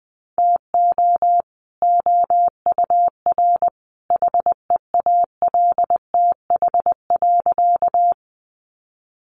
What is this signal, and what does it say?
Morse code, heard as a keyed tone, about 20 wpm, 700 Hz. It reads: TO OUR HEALTH.
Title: Sentences from Top 600 Words 20wpm